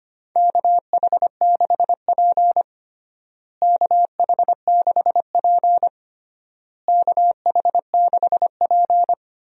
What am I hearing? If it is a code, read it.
Morse code, heard as a keyed tone, about 25 wpm, 700 Hz. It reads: KH6P KH6P KH6P
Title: DXCC-100 Call Signs - 4 Characters - Repeated 3x - 25wpm